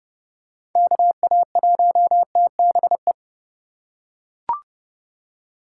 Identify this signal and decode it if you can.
Morse code, heard as a keyed tone, about 31 wpm, 700 Hz. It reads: KA1TBE
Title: USA Call Signs - 6 Characters - 4x Character Spacing - 30wpm